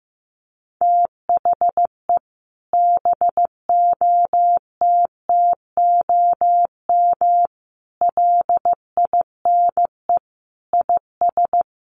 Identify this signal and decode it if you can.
Morse code, heard as a keyed tone, about 15 wpm, 700 Hz. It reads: THE BOTTOM LINE IS
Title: Sets of 5 Words Encore 15wpm